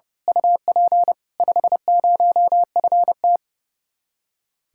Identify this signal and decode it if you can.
Morse code, heard as a keyed tone, about 29 wpm, 700 Hz. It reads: UP 50FT
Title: Speed-Racing Common QSO Phrases - 30wpm